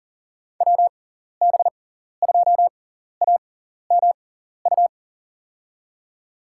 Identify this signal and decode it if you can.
Morse code, heard as a keyed tone, about 40 wpm, spaced as for 14 wpm, 700 Hz. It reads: WB2AMU
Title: USA Call Signs - 6 Characters - 6x Character Spacing - 40wpm